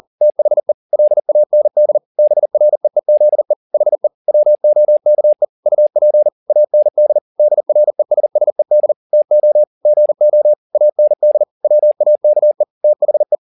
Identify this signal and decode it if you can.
Morse code, heard as a keyed tone, about 40 wpm, 600 Hz. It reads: THE LAND BREEZE HE WOKE UP AND DRESSED TO GO AND WAKE THE